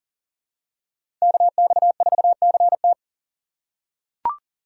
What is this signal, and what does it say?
Morse code, heard as a keyed tone, about 40 wpm, 700 Hz. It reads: KX4CT
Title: USA Call Signs - 6x Character Spacing - 40wpm